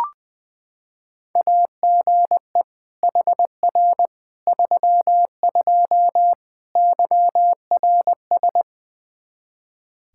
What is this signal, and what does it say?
Morse code, heard as a keyed tone, about 20 wpm, 700 Hz. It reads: AGE HR 32 YRS
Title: Practice QSO - Exchange 3 - More Info - 20wpm